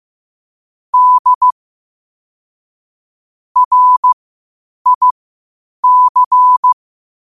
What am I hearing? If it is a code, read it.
Morse code, heard as a keyed tone, about 15 wpm, spaced as for 8 wpm, 1000 Hz. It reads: D RIC
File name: qotdfiles / 8wpm